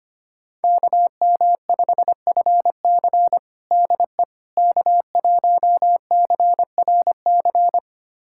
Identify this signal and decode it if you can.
Morse code, heard as a keyed tone, about 25 wpm, 700 Hz. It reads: KM5FC DE K1CRC
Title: USA Callsign DE Callsign - 5 Characters - 25wpm